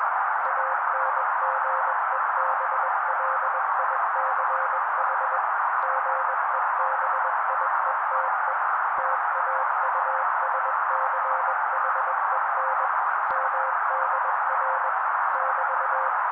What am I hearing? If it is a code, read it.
Morse code, heard as a keyed tone, about 21 wpm, 530 Hz. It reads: ANGEBLICH GEBIETE TAUSCHEN MDR <BT>